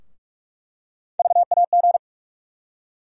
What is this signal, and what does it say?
Morse code, heard as a keyed tone, about 42 wpm, 700 Hz. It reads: VAG
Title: USA Call Sign Postfixes - 3 Letters 45wpm